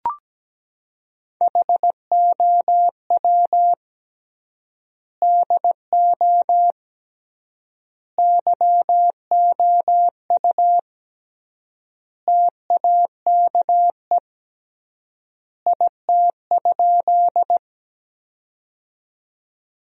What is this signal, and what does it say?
Morse code, heard as a keyed tone, about 17 wpm, 700 Hz. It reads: HOW DO YOU TAKE IT?
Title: 3x Word-Spacing - Sentences from Top 200 Words - 17wpm